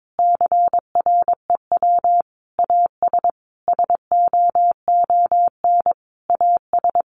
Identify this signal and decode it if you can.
Morse code, heard as a keyed tone, about 22 wpm, 700 Hz. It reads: CREW AS SOON AS
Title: Sentences from Top 1500 Words 22wpm